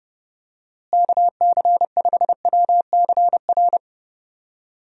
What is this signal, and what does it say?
Morse code, heard as a keyed tone, about 30 wpm, 700 Hz. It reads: KC5WCR
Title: USA Call Signs - 6 Characters - Repeated 2x - 30wpm